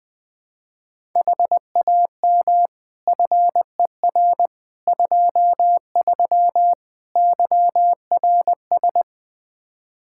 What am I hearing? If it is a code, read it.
Morse code, heard as a keyed tone, about 21 wpm, 700 Hz. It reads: HAM FER 23 YRS
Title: K1USN SST QSO Practice 20wpm